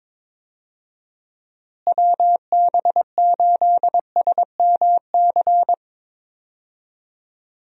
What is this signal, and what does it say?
Morse code, heard as a keyed tone, about 21 wpm, 700 Hz. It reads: WB8SMC
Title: USA Call Signs - 6 Characters - Repeated 2x - 22wpm